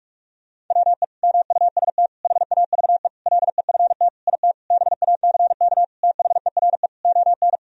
Morse code, heard as a keyed tone, about 41 wpm, 700 Hz. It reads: WE MUST HAVE LEFT IT BACK THERE ON
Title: Sentences from Top 1000 Words - Review Part 2 - 45wpm